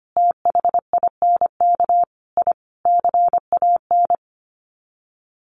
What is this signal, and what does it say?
Morse code, heard as a keyed tone, about 25 wpm, 700 Hz. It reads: THINK I CAN
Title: Sets of 5 Words Encore2 25wpm